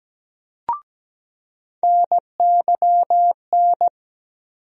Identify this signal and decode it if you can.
Morse code, heard as a keyed tone, about 17 wpm, 700 Hz. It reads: NYN
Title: USA Call Sign Postfixes - 3 Letters 17wpm